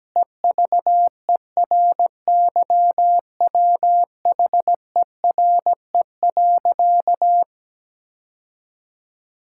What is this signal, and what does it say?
Morse code, heard as a keyed tone, about 17 wpm, 700 Hz. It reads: EVERYWHERE.
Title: Sentences from Top 1600 Words 17wpm